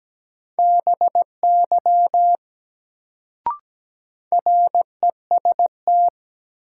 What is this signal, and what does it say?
Morse code, heard as a keyed tone, about 17 wpm, 700 Hz. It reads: BY REST
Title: Sets of 2 Words - New Word Drill - 17wpm